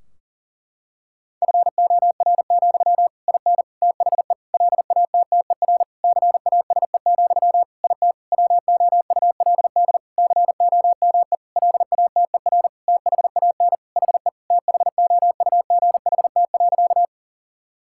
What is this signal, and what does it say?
Morse code, heard as a keyed tone, about 40 wpm, 700 Hz. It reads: FOR, IN THE LATTER CASE, IT WOULD COME LATER THAN HE THOUGHT.
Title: Sentences from Top 1600 Words 40wpm